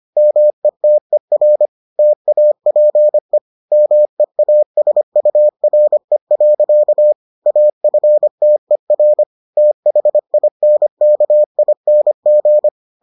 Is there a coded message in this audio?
Yes: Morse code from 0.2 to 12.7 s, heard as a keyed tone, about 25 wpm, 600 Hz. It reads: METER TAPE MEASURE. AFTER THINKING